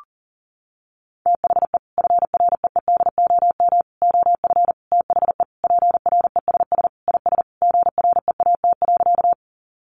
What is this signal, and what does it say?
Morse code, heard as a keyed tone, about 40 wpm, 700 Hz. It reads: THE FREEDOM OF THE PRESS IS GREAT.
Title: Sentences from Top 800 Words 40wpm